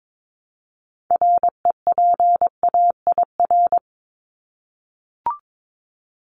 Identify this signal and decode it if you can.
Morse code, heard as a keyed tone, about 22 wpm, 700 Hz. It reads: REPAIR
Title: Top 1501-1750 Words 22wpm